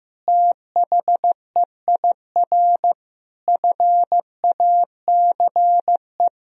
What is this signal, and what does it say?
Morse code, heard as a keyed tone, about 15 wpm, 700 Hz. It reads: THEIR FACE
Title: Sentences from Top 1000 Words - Review Part 1 - 15wpm